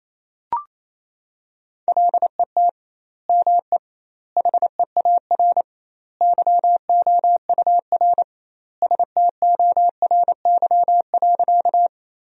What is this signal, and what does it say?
Morse code, heard as a keyed tone, about 28 wpm, 700 Hz. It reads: LET ME HEAR YOUR STORY.
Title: 2x Word-Spacing - Sentences from Top 300 Words - 28wpm